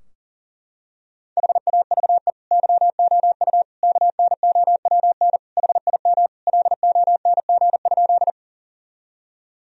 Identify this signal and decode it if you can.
Morse code, heard as a keyed tone, about 40 wpm, 700 Hz. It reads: HAVE YOU KNOWN HIM LONG?
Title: Sentences from Top 1000 Words - Review Part 3 - 40wpm